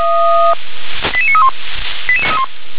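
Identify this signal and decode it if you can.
Five-tone selective call (ZVEI), heard as quick runs of five tones, tones of about 70 ms. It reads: 88931
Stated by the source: ZVEI five-tone call received on an SDR.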